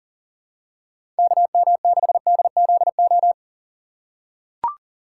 Morse code, heard as a keyed tone, about 40 wpm, 700 Hz. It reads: KM6DZO
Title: USA Call Signs - 6 Characters - 2x Character Spacing - 40wpm